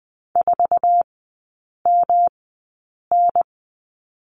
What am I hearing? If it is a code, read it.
Morse code, heard as a keyed tone, about 20 wpm, spaced as for 8 wpm, 700 Hz. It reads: 4MN
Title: 107 Lesson 18e - Letter m - Three Characters 20wpm